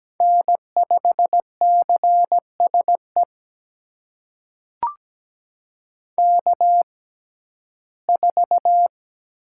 Morse code, heard as a keyed tone, about 17 wpm, spaced as for 12 wpm, 700 Hz. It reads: N5CSE K4
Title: USA Call Signs - 5 Characters - 6x Character Spacing - 17wpm